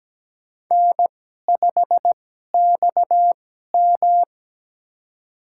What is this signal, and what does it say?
Morse code, heard as a keyed tone, about 17 wpm, spaced as for 12 wpm, 700 Hz. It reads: N5XM
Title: USA Call Signs - 2x Character Spacing - 17wpm